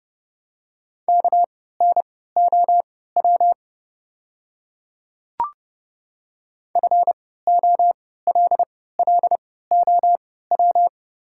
Morse code, heard as a keyed tone, about 30 wpm, spaced as for 17 wpm, 700 Hz. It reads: KNOW FOLLOW
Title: 3x Character Spacing - Top 300 Words - 30wpm